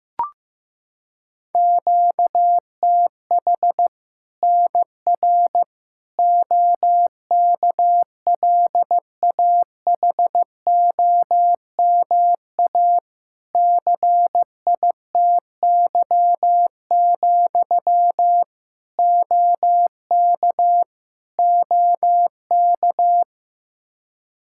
Morse code, heard as a keyed tone, about 16 wpm, 700 Hz. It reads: QTH NR OKLAHOMA CITY, OK OK
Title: Practice QSO - Exchange 1 - Beginning - 15wpm